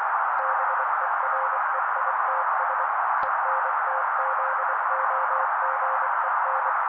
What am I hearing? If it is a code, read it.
Morse code, heard as a keyed tone, about 24 wpm, 530 Hz. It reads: BEREITS ENTZOGEN
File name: daily / 23wpm noise